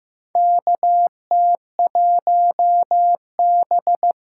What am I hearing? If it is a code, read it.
Morse code, heard as a keyed tone, about 15 wpm, 700 Hz. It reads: KT1B
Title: K1USN SST QSO Practice - 15wpm